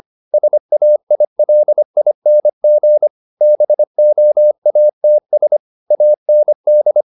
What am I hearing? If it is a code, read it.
Morse code, heard as a keyed tone, about 25 wpm, 600 Hz. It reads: SAILING BOATS AND